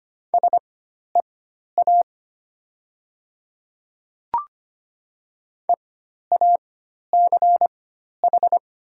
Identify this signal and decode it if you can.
Morse code, heard as a keyed tone, about 25 wpm, spaced as for 12 wpm, 690 Hz. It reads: SEA EACH
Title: 4x Character Spacing - Top 500 Words - 25wpm